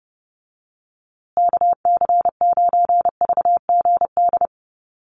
Morse code, heard as a keyed tone, about 30 wpm, 700 Hz. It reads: KC9VGD
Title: USA Call Signs Repeated 3x 30wpm